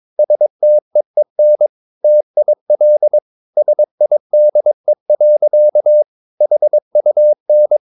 Morse code, heard as a keyed tone, about 22 wpm, 600 Hz. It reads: STEEN TIL SIDE. HUN
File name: Ferie QRS 22wpm